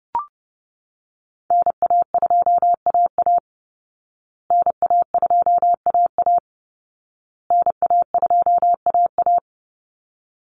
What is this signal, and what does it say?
Morse code, heard as a keyed tone, about 30 wpm, 700 Hz. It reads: NA2AA NA2AA NA2AA
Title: Call Signs Repeated 2x - 30wpm